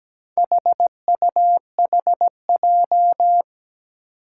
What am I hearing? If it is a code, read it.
Morse code, heard as a keyed tone, about 17 wpm, 700 Hz. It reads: HUHJ
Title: Four Challenging Letter-Number Combinations 17wpm